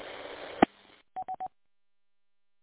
There is no code.